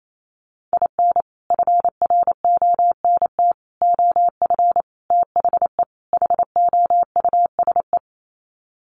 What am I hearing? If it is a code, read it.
Morse code, heard as a keyed tone, about 28 wpm, 700 Hz. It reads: IN FRONT OF THE HOUSE
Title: Sets of 5 Words Encore2 28wpm